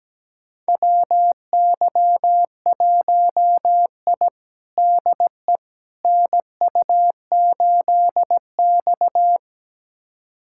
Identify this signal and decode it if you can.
Morse code, heard as a keyed tone, about 17 wpm, 700 Hz. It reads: WY1I DE NU8X
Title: USA Callsign DE Callsign - 4 Characters - 17wpm